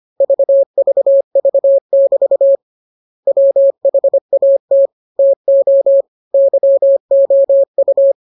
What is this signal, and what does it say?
Morse code, heard as a keyed tone, about 25 wpm, 550 Hz. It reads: VVV<BT> WHAT TO YOU
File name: jokes / joke 8 25 0000